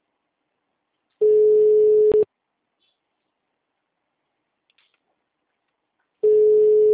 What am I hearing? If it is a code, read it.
Ringback tone (Europe, 425 Hz).
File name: out- -2010- - - .66040